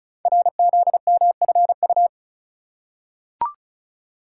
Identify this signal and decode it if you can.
Morse code, heard as a keyed tone, about 36 wpm, 700 Hz. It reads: RZMFU
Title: Code Groups - 5 Letter:Numbers - 35wpm